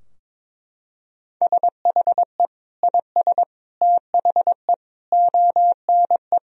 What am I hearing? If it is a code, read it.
Morse code, heard as a keyed tone, about 22 wpm, 700 Hz. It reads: SHE IS THE ONE
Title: Sentences from Top 1000 Words - Review Part 2 - 22wpm